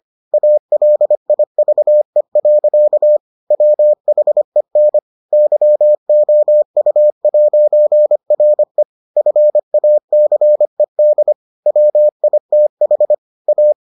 Morse code, heard as a keyed tone, about 25 wpm, 600 Hz. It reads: ALIVE. WHEN YOU'RE FACED WITH A